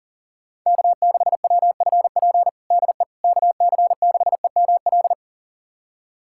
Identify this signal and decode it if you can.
Morse code, heard as a keyed tone, about 40 wpm, 700 Hz. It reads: K6WFP DE KC6EML
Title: USA Callsign DE Callsign 40wpm